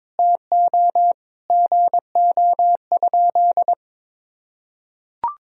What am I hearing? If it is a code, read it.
Morse code, heard as a keyed tone, about 22 wpm, 700 Hz. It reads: TO GO?
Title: Sentences from Top 1000 Words - Review Part 3 - 22wpm